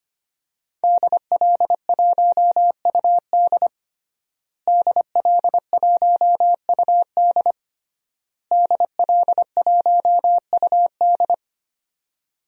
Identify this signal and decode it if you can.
Morse code, heard as a keyed tone, about 25 wpm, 700 Hz. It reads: DL1UD DL1UD DL1UD
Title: DXCC-100 Call Signs - 5 Characters - Repeated 3x - 25wpm